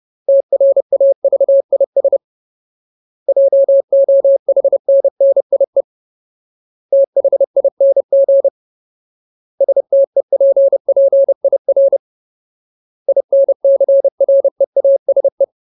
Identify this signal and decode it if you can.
Morse code, heard as a keyed tone, about 30 wpm, 550 Hz. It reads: TRAVIS JOHNNIE THING STEPPIR INCREASE
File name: Ultimate Word List / 30WPM Ultimate